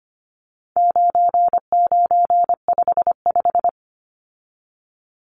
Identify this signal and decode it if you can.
Morse code, heard as a keyed tone, about 25 wpm, 700 Hz. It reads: 9955
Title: Numbers - 4-Digit 25wpm